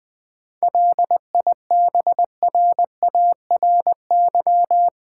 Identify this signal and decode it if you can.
Morse code, heard as a keyed tone, about 20 wpm, 700 Hz. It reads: LIBRARY
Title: Sentences from Top 600 Words - New Word Drill - 20wpm